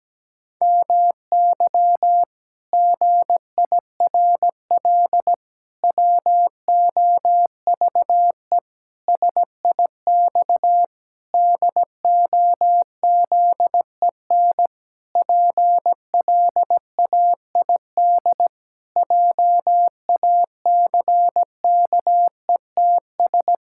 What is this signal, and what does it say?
Morse code, heard as a keyed tone, about 17 wpm, 700 Hz. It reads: MY GIRL WOVE SIX DOZEN PLAID JACKETS
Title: Pangram Challenge v2 - 17wpm